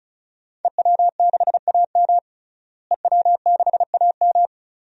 Morse code, heard as a keyed tone, about 36 wpm, 700 Hz. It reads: EW6AM EW6AM
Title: DXCC-100 Call Signs - 5 Characters - Repeated 3x - 35wpm